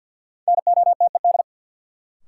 Morse code, heard as a keyed tone, about 53 wpm, 700 Hz. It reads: NOTED
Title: Sets of 5 Words - New Word Drill - 50wpm